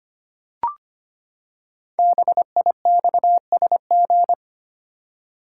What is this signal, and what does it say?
Morse code, heard as a keyed tone, about 25 wpm, 700 Hz. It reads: BIXSG
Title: Code Groups - 5 Letters - 25wpm